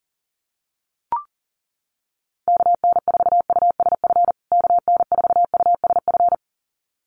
Morse code, heard as a keyed tone, about 40 wpm, 700 Hz. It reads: KN4USF KN4USF
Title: USA Call Signs Repeated 2x 40wpm